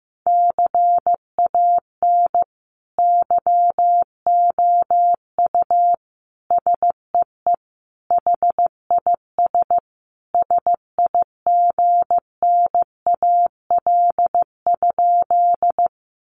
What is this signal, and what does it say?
Morse code, heard as a keyed tone, about 15 wpm, 700 Hz. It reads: CAN YOU SEE HIS SIGNAL?
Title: Sentences from Top 1400 Words 15wpm